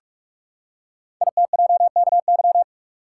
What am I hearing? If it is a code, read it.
Morse code, heard as a keyed tone, about 44 wpm, 700 Hz. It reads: ITJKY
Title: Code Groups - 5 Letters - 45wpm